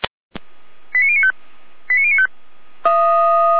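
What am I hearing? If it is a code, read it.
Five-tone selective call (ZVEI), heard as quick runs of five tones, tones of about 70 ms. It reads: 89095 89095
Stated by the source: ZVEI five-tone call received by SDR, from the Feuerwehr Orlamünde (fire brigade) archive.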